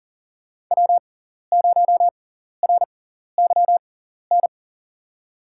Morse code, heard as a keyed tone, about 40 wpm, spaced as for 14 wpm, 700 Hz. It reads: W0RYN
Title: USA Call Signs - 5 Characters - 6x Character Spacing - 40wpm